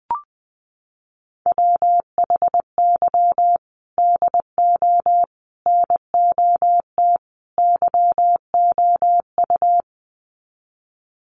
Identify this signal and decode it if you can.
Morse code, heard as a keyed tone, about 20 wpm, 700 Hz. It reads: WHY DO NOT YOU
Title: Sets of 4 Words 20wpm